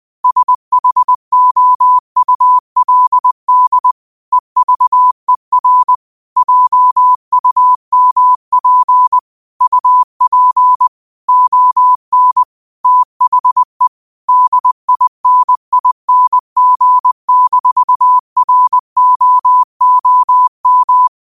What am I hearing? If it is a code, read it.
Morse code, heard as a keyed tone, about 20 wpm, 1000 Hz. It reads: SHOULD EVER JUMP UP ON THE DINING-ROOM
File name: qotdfiles / 20wpm